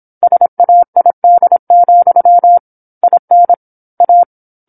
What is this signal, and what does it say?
Morse code, heard as a keyed tone, about 27 wpm, 700 Hz. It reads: SAID, IN A
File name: 08-Chapter-08 26 WPM